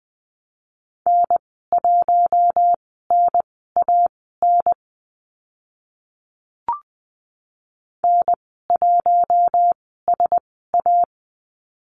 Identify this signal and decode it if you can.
Morse code, heard as a keyed tone, about 20 wpm, spaced as for 15 wpm, 700 Hz. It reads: N1NAN N1SA
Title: 042 Lesson 8g - Number 1 - Call Signs 20wpm